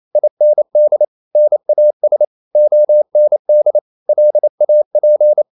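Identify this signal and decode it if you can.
Morse code, heard as a keyed tone, about 28 wpm, 600 Hz. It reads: IND NAS OND LAP
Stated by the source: Danish short words sent in Morse at 28 wpm, from CW med Gnister.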